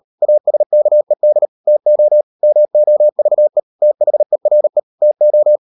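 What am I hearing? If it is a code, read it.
Morse code, heard as a keyed tone, about 38 wpm, 600 Hz. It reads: ASKED TO MOVE THERE TO
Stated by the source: Morse code practice at 38 WPM, sending Ken Follett's novel Never in English, from CW med Gnister.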